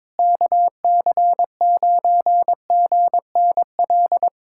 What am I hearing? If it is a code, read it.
Morse code, heard as a keyed tone, about 22 wpm, 700 Hz. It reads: KC9GNL
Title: USA Callsign DE Callsign 22wpm